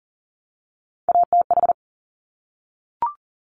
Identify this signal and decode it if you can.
Morse code, heard as a keyed tone, about 39 wpm, 700 Hz. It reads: ATH
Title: Top Three Letter Combinations in Words 40wpm